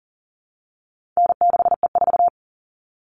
Morse code, heard as a keyed tone, about 40 wpm, 700 Hz. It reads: N6E4
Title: All Cut Numbers - 4-Digit 40wpm